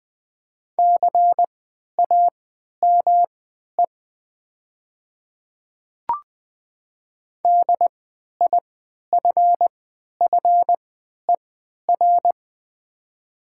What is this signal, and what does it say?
Morse code, heard as a keyed tone, about 20 wpm, spaced as for 11 wpm, 700 Hz. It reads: CAME DIFFER
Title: 3x Character Spacing - Top 200 Words - 20wpm